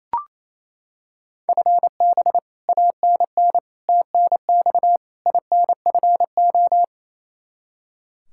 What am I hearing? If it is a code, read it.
Morse code, heard as a keyed tone, about 28 wpm, 700 Hz. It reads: FB ANN TNX INFO
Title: Practice QSO - Exchange 3 - More Info - 28wpm